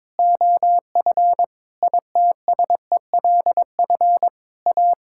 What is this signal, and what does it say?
Morse code, heard as a keyed tone, about 22 wpm, 700 Hz. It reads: OF ITSELF A
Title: Sentences from Top 1000 Words - Review Part 5 - 22wpm